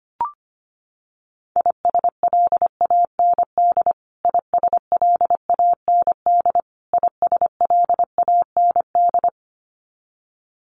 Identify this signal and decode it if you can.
Morse code, heard as a keyed tone, about 25 wpm, 700 Hz. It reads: ISLAND ISLAND ISLAND
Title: Top 500 Words Repeated 25wpm